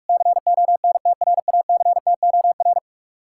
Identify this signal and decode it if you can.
Morse code, heard as a keyed tone, about 66 wpm, spaced as for 55 wpm, 700 Hz. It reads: CONTRACTOR
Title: Speed-Racing Common QSO Phrases - 45wpm